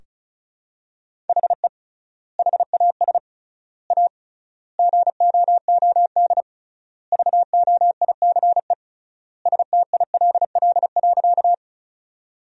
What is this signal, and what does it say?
Morse code, heard as a keyed tone, about 36 wpm, 700 Hz. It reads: HE HAS A GOOD VOICE STILL.
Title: 3x Word-Spacing - Sentences from Top 500 Words - 35wpm